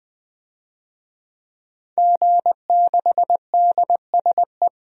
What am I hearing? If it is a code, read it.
Morse code, heard as a keyed tone, about 21 wpm, 700 Hz. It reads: G6DSE
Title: DXCC-100 Call Signs - 5 Characters - 20wpm